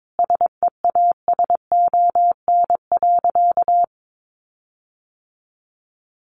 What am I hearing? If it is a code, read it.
Morse code, heard as a keyed tone, about 22 wpm, 700 Hz. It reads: SEASON.
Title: Sentences from Top 1400 Words 22wpm